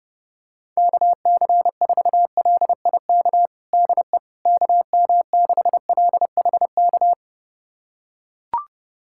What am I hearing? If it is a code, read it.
Morse code, heard as a keyed tone, about 32 wpm, 700 Hz. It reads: KC4LIK DE KM6LHK
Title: USA Callsign DE Callsign - 6 Characters - 30wpm